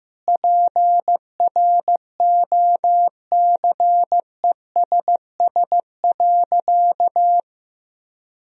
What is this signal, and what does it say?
Morse code, heard as a keyed tone, about 15 wpm, 700 Hz. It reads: PROCESS.
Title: Sentences from Top 1000 Words - Review Part 3 - 15wpm